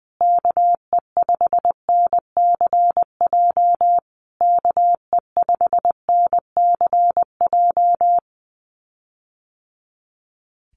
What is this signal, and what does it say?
Morse code, heard as a keyed tone, about 20 wpm, 700 Hz. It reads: KE5NCJ KE5NCJ
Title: USA Call Signs Repeated 3x 20wpm